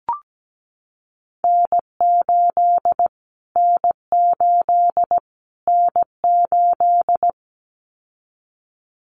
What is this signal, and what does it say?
Morse code, heard as a keyed tone, about 17 wpm, 700 Hz. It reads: N8 N8 N8
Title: Callsign Prefixes for Top 25 DXCC - Repeated 3x - 17wpm